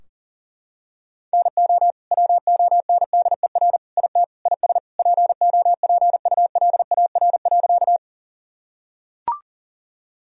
Morse code, heard as a keyed tone, about 40 wpm, 700 Hz. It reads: NO WONDER IT IS POPULAR.
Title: Sentences from Top 1000 Words - Review Part 3 - 40wpm